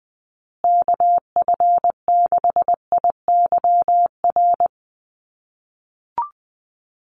Morse code, heard as a keyed tone, about 20 wpm, 700 Hz. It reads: KF6IYR
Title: USA Call Signs Repeated 3x 20wpm